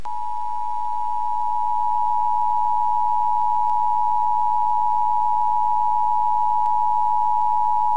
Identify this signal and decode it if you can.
Emergency Alert System (SAME) tones (the 853 and 960 Hz attention tone). It is heard from 0.2 to 7.7 s.